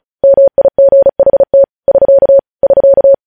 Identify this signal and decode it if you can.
Morse code, heard as a keyed tone, about 36 wpm, 550 Hz. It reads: MIGHT <SK> <SK>
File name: podcast / News headlines-1x-35x35